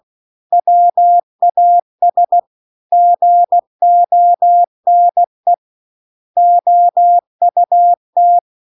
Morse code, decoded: WAS GONE OUT

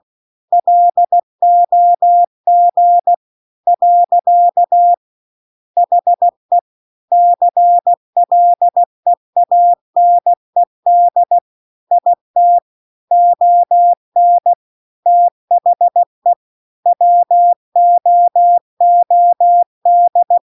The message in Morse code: LOG . HE CLEANED IT ON THE WOOD

